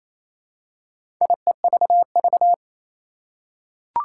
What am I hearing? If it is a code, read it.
Morse code, heard as a keyed tone, about 28 wpm, 700 Hz. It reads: IEVV
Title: Four Challenging Letter-Number Combinations 28wpm